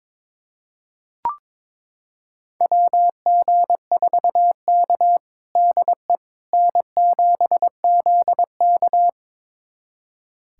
Morse code, heard as a keyed tone, about 22 wpm, 700 Hz. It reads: WG4K DE N7ZK
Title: USA Callsign DE Callsign - 4 Characters - 22wpm